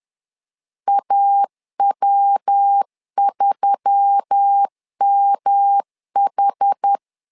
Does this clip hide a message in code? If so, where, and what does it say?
Yes: Morse code from 0.9 to 7.0 s, heard as a keyed tone, about 10 wpm, 800 Hz. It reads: AW3MH